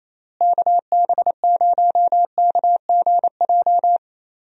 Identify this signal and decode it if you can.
Morse code, heard as a keyed tone, about 28 wpm, 700 Hz. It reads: KB0KGJ
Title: USA Call Signs - 6 Characters - Repeated 3x - 28wpm